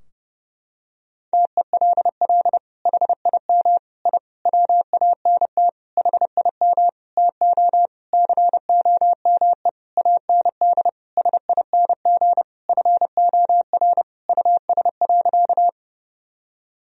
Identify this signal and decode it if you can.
Morse code, heard as a keyed tone, about 30 wpm, 700 Hz. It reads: TELL HIM I WANT HIM TO COME AND SING FOR US.
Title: Sentences from Top 1000 Words - Review Part 2 - 30wpm